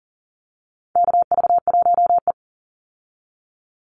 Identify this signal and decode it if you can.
Morse code, heard as a keyed tone, about 40 wpm, 700 Hz. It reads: KV1E
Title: Call Signs Repeated 2x - 40wpm